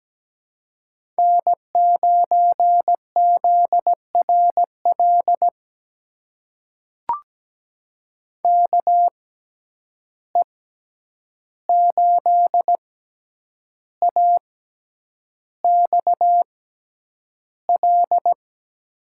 Morse code, heard as a keyed tone, about 17 wpm, spaced as for 8 wpm, 700 Hz. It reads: N9ZRL KE8AXL
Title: USA Call Signs - 6x Character Spacing - 17wpm